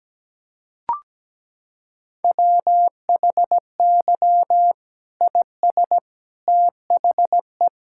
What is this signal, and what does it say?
Morse code, heard as a keyed tone, about 17 wpm, 700 Hz. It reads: WHY IS THE
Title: Sentences from Top 1600 Words 17wpm